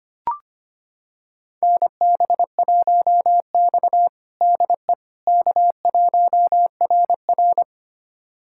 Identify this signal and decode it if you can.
Morse code, heard as a keyed tone, about 25 wpm, 700 Hz. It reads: NB1X DE K1RR
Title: USA Callsign DE Callsign - 4 Characters - 25wpm